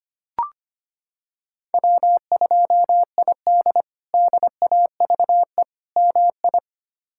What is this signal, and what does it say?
Morse code, heard as a keyed tone, about 25 wpm, 700 Hz. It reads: W2ID DAVE MI
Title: K1USN SST QSO Practice - 25wpm